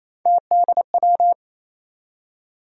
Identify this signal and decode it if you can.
Morse code, heard as a keyed tone, about 28 wpm, 700 Hz. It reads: TDW
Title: USA Call Sign Postfixes - 3 Letters 28wpm